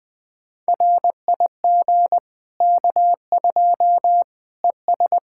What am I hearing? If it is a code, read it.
Morse code, heard as a keyed tone, about 20 wpm, 700 Hz. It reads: RIG K2 ES
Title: Practice QSO - Exchange 2 - Info - 20wpm